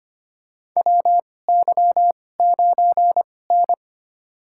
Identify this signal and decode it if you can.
Morse code, heard as a keyed tone, about 25 wpm, spaced as for 18 wpm, 700 Hz. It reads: WY9N
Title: USA Call Signs - 2x Character Spacing - 25wpm